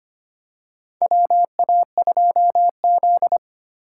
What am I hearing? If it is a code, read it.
Morse code, heard as a keyed tone, about 25 wpm, 700 Hz. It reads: WA2Z
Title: USA Call Signs - 4 Characters - 4x Character Spacing - 25wpm